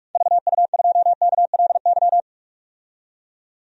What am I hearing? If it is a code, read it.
Morse code, heard as a keyed tone, about 44 wpm, 700 Hz. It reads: VU2KLY
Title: DXCC-100 Call Signs - 6 Characters - 45wpm